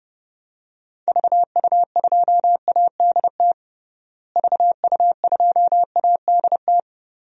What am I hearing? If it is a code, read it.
Morse code, heard as a keyed tone, about 30 wpm, 700 Hz. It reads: VU2ADT VU2ADT
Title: DXCC-100 Call Signs - 6 Characters - Repeated 2x - 30wpm